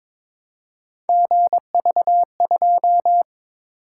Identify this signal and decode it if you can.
Morse code, heard as a keyed tone, about 22 wpm, 700 Hz. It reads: GV2
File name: practice / 22wpm / Three Letter-Number Combinations - 22wpm